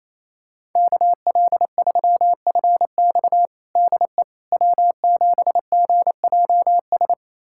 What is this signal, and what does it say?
Morse code, heard as a keyed tone, about 28 wpm, 700 Hz. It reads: KL3FX DE W7GJS
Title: USA Callsign DE Callsign - 5 Characters - 28wpm